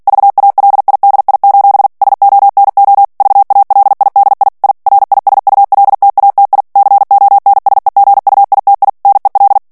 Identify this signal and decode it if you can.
Morse code, heard as a keyed tone, about 48 wpm, 800 Hz. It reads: VALIDI: SONO VALIDI I RISULTATI CONSEGUITI NEL
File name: ITALIEN texte clair 2 vitesse 45 mots minute